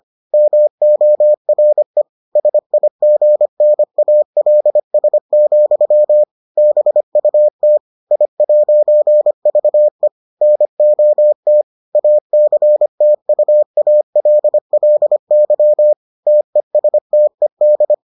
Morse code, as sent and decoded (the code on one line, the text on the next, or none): -- --- .-. / ... .. --. -. .- .-.. ... --..-- / -... ..- - / .. .----. ...- . / -. --- - / .- -.-. - ..- .- .-.. .-.. -.-- / - . ... - . -..
MORE SIGNALS, BUT I'VE NOT ACTUALLY TESTED